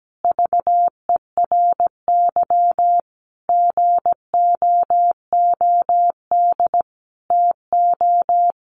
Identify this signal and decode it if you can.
Morse code, heard as a keyed tone, about 17 wpm, 700 Hz. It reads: VERY GOOD TO
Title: Sentences from Top 200 Words 17wpm